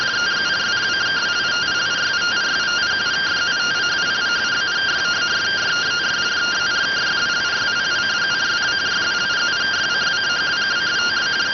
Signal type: two-tone teleprinter signal

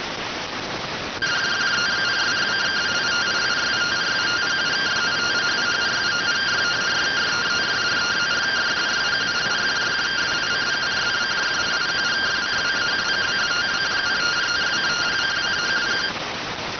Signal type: two-tone teleprinter signal

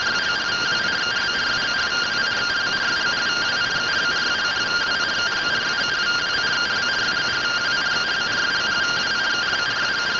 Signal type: two-tone teleprinter signal

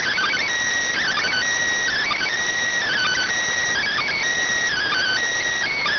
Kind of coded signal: Emergency Alert System (SAME) tones

